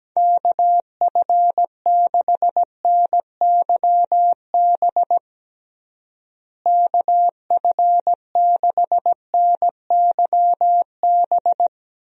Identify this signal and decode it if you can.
Morse code, heard as a keyed tone, about 17 wpm, 700 Hz. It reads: KF6NYB KF6NYB
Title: USA Call Signs - 6 Characters - Repeated 3x - 17wpm